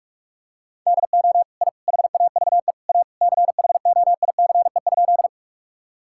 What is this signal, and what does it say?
Morse code, heard as a keyed tone, about 48 wpm, 700 Hz. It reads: DO I HAVE A CHOICE?
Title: Sentences from Top 1000 Words - Review Part 6 - 45wpm